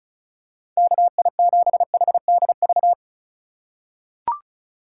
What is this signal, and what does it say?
Morse code, heard as a keyed tone, about 36 wpm, 700 Hz. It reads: KI7HDV
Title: USA Call Signs - 6 Characters - 6x Character Spacing - 35wpm